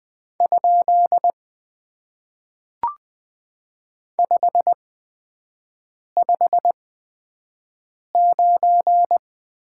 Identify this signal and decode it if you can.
Morse code, heard as a keyed tone, about 20 wpm, spaced as for 5 wpm, 700 Hz. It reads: ? 559
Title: Top 90 QSO Elements - 8x Character Spacing - 20wpm